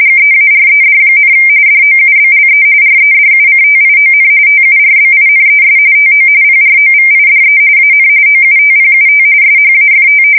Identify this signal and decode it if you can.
RTTY, about 45.45 baud. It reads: GEARED, DOUBLE REDUCTION TURBINES TO PROVIDE AN OUTPUT SHAFT
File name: fsk170 / battleshipnj